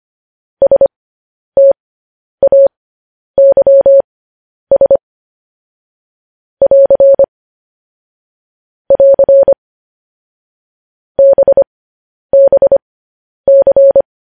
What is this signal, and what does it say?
Morse code, heard as a keyed tone, about 25 wpm, spaced as for 10 wpm, 550 Hz. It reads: STAYS <AR> <AR> BBC
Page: News Headlines in Morse Code at 10 WPM